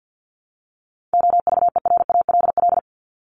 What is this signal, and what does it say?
Morse code, heard as a keyed tone, about 50 wpm, 700 Hz. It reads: OVERALL
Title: Top 751-1000 Words 50wpm